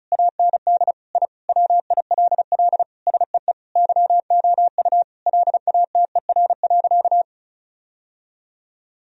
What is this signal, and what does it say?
Morse code, heard as a keyed tone, about 34 wpm, 700 Hz. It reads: AND I WILL SEE YOU LATER.
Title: Sentences from Top 1000 Words - Review Part 4 - 35wpm